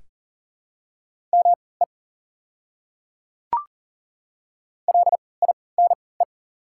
Morse code, heard as a keyed tone, about 40 wpm, spaced as for 23 wpm, 690 Hz. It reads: ME LINE
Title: 3x Character Spacing - Top 200 Words - 40wpm